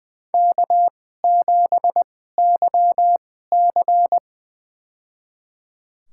Morse code, heard as a keyed tone, about 20 wpm, spaced as for 14 wpm, 700 Hz. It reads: K7YC
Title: USA Call Signs - 4 Characters - 2x Character Spacing - 20wpm